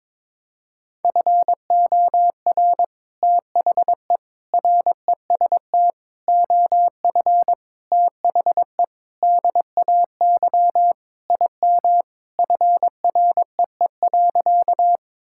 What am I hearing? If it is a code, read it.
Morse code, heard as a keyed tone, about 22 wpm, 700 Hz. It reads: FOR THE REST OF THE DAY IM FREE.
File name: practice / 22wpm / Sentences with 3-Word Phrases - 22wpm